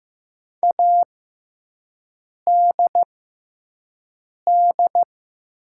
Morse code, heard as a keyed tone, about 15 wpm, spaced as for 5 wpm, 700 Hz. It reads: ADD
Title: 6x Character Spacing - Top 500 Words - 15wpm